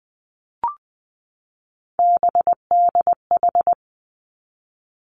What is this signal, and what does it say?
Morse code, heard as a keyed tone, about 19 wpm, 700 Hz. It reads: BDH
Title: Three Challenging Letter-Number Combinations 20wpm